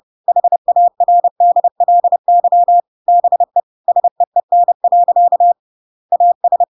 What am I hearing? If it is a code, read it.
Morse code, heard as a keyed tone, about 30 wpm, 700 Hz. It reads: HARDLY BE SEEN. AS
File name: Pinocchio / Pinocchio ch8 30 30 ik1hge